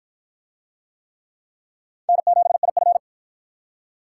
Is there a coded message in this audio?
Yes: Morse code from 2.1 to 3.0 s, heard as a keyed tone, about 45 wpm, 700 Hz. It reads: N7IF